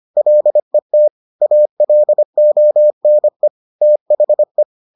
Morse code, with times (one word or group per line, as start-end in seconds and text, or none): 0.2-1.1 LET
1.4-3.5 ALONE
3.8-4.6 THE